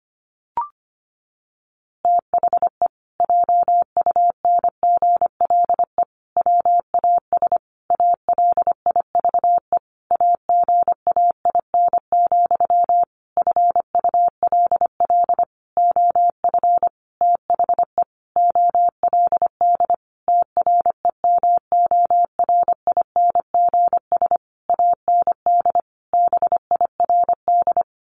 Morse code, heard as a keyed tone, about 25 wpm, 700 Hz. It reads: THE JUNGLE WAS ALIVE AGAIN, FULL OF THE OLD TREMORINGS AND BIRD
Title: A Sound Of Thunder 25wpm